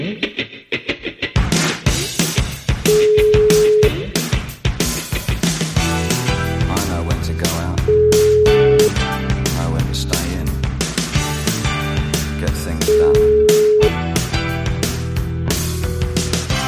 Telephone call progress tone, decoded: ringback tone

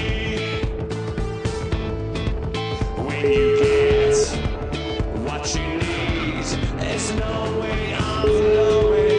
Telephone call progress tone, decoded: ringback tone

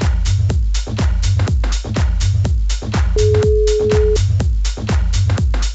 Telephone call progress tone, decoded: ringback tone

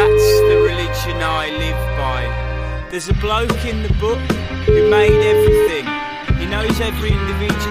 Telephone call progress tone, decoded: ringback tone

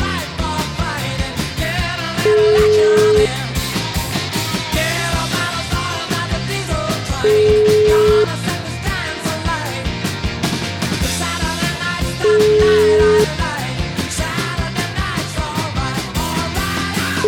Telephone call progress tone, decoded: ringback tone

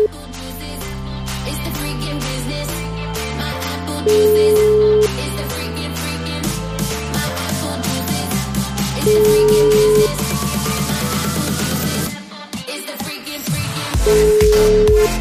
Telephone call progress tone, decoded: ringback tone

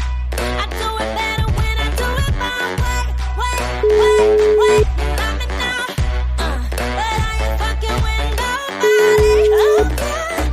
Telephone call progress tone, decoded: ringback tone

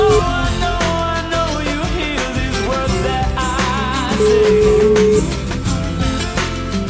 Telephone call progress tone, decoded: ringback tone

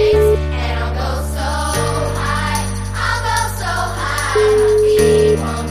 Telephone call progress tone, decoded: ringback tone

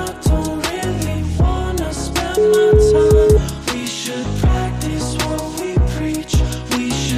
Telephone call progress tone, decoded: ringback tone

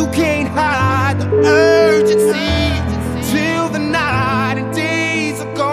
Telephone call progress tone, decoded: ringback tone

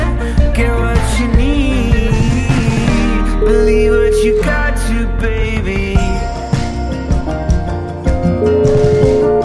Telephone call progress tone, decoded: ringback tone

